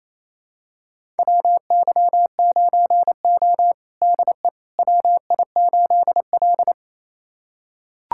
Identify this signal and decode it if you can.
Morse code, heard as a keyed tone, about 28 wpm, 700 Hz. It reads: WY9O DE WI8L
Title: USA Callsign DE Callsign - 4 Characters - 28wpm